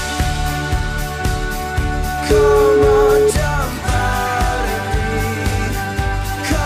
Ringback tone (Europe, 425 Hz), with noise.